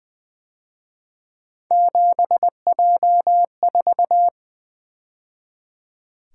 Morse code, heard as a keyed tone, about 24 wpm, 700 Hz. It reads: 7J4
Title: Speed-Racing - Three Letter-Number Combinations - 20wpm